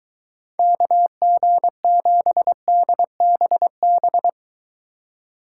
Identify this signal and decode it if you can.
Morse code, heard as a keyed tone, about 23 wpm, 700 Hz. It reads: KG7DBB